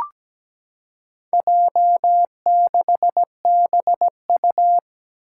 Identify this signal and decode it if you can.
Morse code, heard as a keyed tone, about 17 wpm, 700 Hz. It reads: J6BU
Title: Four Challenging Letter-Number Combinations 17wpm